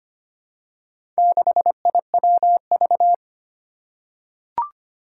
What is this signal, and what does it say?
Morse code, heard as a keyed tone, about 25 wpm, 700 Hz. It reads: 6IWV
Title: Four Challenging Letter-Number Combinations 25wpm